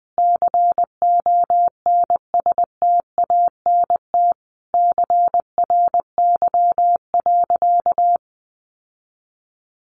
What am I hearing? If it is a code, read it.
Morse code, heard as a keyed tone, about 20 wpm, 700 Hz. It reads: CONSTANT CRY.
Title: Sentences from Top 1500 Words 20wpm